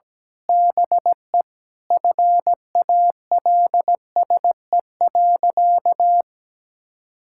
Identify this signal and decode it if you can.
Morse code, heard as a keyed tone, about 17 wpm, 700 Hz. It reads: BE FALSE.
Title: Sentences from Top 1600 Words 17wpm